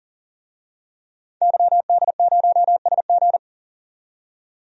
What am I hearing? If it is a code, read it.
Morse code, heard as a keyed tone, about 37 wpm, 700 Hz. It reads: YD0SG
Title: DXCC-100 Call Signs - 5 Characters - 40wpm